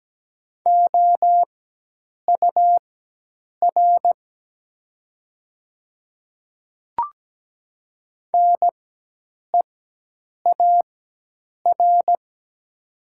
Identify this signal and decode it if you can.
Morse code, heard as a keyed tone, about 17 wpm, spaced as for 8 wpm, 700 Hz. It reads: OUR NEAR
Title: 4x Character Spacing - Top 200 Words - 17wpm